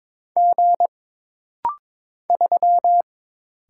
Morse code, heard as a keyed tone, about 22 wpm, 700 Hz. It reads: G 3
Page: This rapid-fire timing is geared to push your ICR (Instant Character Recognition) skills.